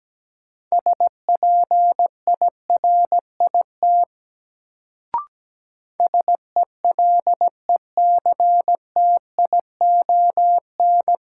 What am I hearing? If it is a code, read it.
Morse code, heard as a keyed tone, about 17 wpm, 700 Hz. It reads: SPIRIT SELECTION
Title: Sentences from Top 1400 Words - New Word Drill Alt - 17wpm